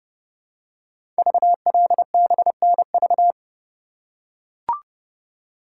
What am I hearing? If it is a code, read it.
Morse code, heard as a keyed tone, about 29 wpm, 700 Hz. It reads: VLBNV
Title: Code Groups - 5 Letter:Numbers - 30wpm